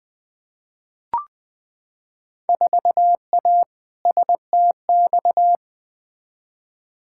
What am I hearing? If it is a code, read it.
Morse code, heard as a keyed tone, about 21 wpm, 700 Hz. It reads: 4A STX
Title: Field Day Exchanges Encore - 20wpm